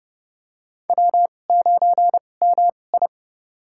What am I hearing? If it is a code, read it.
Morse code, heard as a keyed tone, about 30 wpm, spaced as for 22 wpm, 700 Hz. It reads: W9MI
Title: USA Call Signs - 4 Characters - 2x Character Spacing - 30wpm